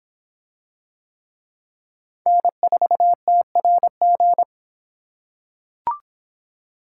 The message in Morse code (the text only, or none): N4TRG